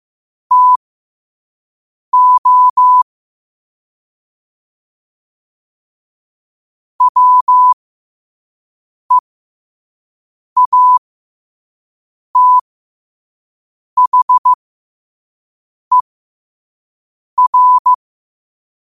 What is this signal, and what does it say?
Morse code, heard as a keyed tone, about 15 wpm, spaced as for 5 wpm, 1000 Hz. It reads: TO WEATHER
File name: qotdfiles / 5wpm